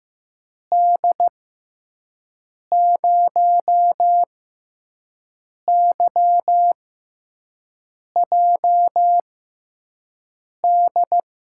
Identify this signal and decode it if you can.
Morse code, heard as a keyed tone, about 15 wpm, spaced as for 5 wpm, 700 Hz. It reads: D0YJD
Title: USA Call Signs - 6x Character Spacing - 15wpm